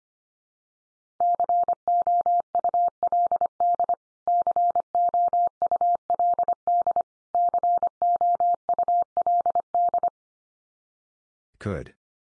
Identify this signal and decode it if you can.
Morse code, heard as a keyed tone, about 25 wpm, 700 Hz. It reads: COULD COULD COULD
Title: Top 100 Words 25wpm x3